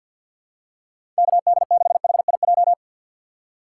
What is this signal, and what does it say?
Morse code, heard as a keyed tone, about 47 wpm, 700 Hz. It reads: KD6HIJ
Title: USA Call Signs - 3x Character Spacing - 50wpm